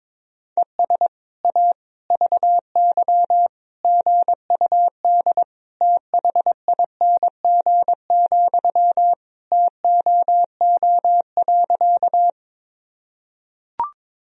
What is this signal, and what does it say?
Morse code, heard as a keyed tone, about 22 wpm, 700 Hz. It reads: ES A VY GUD THING, TOO.
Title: Sentences from Top 200 Words and Abbreviations 22wpm